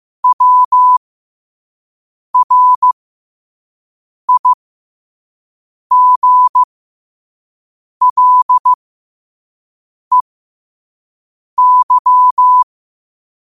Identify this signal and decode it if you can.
Morse code, heard as a keyed tone, about 15 wpm, spaced as for 5 wpm, 1000 Hz. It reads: WRIGLEY
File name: qotdfiles / 5wpm